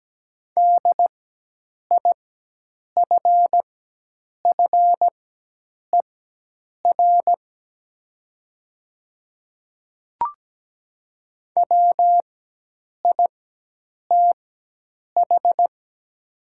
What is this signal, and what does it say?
Morse code, heard as a keyed tone, about 17 wpm, spaced as for 8 wpm, 700 Hz. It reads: DIFFER WITH
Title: 4x Character Spacing - Top 200 Words - 17wpm